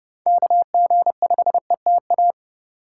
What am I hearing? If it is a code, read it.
Morse code, heard as a keyed tone, about 29 wpm, 700 Hz. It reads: KG5ETA